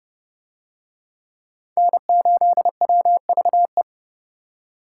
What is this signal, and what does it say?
Morse code, heard as a keyed tone, about 28 wpm, 700 Hz. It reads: N8WVE